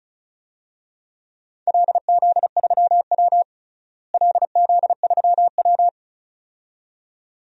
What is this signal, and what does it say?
Morse code, heard as a keyed tone, about 36 wpm, 700 Hz. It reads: LZ3W LZ3W
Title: DXCC-100 Call Signs - 4 Characters - Repeated 2x - 35wpm